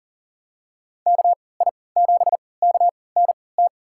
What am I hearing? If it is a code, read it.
Morse code, heard as a keyed tone, about 40 wpm, spaced as for 23 wpm, 700 Hz. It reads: KI7KNT
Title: USA Call Signs - 6 Characters - 3x Character Spacing - 40wpm